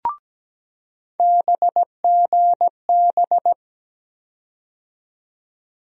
Morse code, heard as a keyed tone, about 17 wpm, 700 Hz. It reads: BGB
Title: USA Call Sign Postfixes - 3 Letters 17wpm